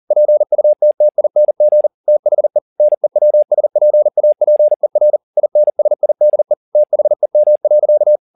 Morse code, heard as a keyed tone, about 40 wpm, 600 Hz. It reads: PUTTING THE NEWSPAPER INSIDE THEM.